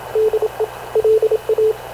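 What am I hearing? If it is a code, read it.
Morse code, heard as a keyed tone, about 27 wpm, 430 Hz, with radio noise. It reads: DE LA
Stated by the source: Havana Radio / CLA sending Spanish press (PX) in Morse, recorded by Radio Officer Paul Dumesnil onboard CNAV QUEST/CZDO.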